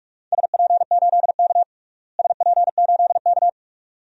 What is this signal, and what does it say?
Morse code, heard as a keyed tone, about 46 wpm, 700 Hz. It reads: SP8K SP8K
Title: DXCC-100 Call Signs - 4 Characters - Repeated 2x - 45wpm